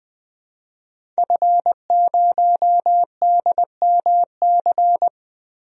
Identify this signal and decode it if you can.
Morse code, heard as a keyed tone, about 20 wpm, 700 Hz. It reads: F0DMC
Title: DXCC-100 Call Signs - 5 Characters - 20wpm